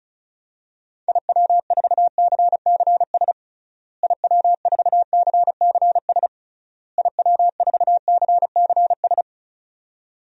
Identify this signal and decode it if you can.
Morse code, heard as a keyed tone, about 36 wpm, 700 Hz. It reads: IW4CCS IW4CCS IW4CCS
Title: DXCC-100 Call Signs - 6 Characters - Repeated 3x - 35wpm